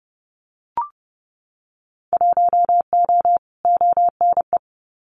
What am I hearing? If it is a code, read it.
Morse code, heard as a keyed tone, about 30 wpm, 700 Hz. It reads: 1O ONE
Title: Winter Field Day Exchanges 30wpm